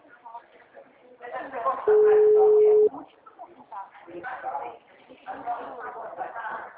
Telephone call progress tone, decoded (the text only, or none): ringback tone